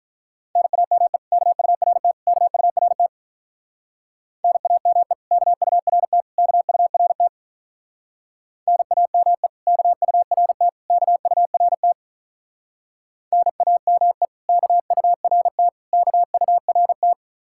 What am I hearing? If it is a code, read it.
Morse code, heard as a keyed tone, about 46 wpm, 700 Hz. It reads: NAME KURT KURT NAME KURT KURT NAME KURT KURT NAME KURT KURT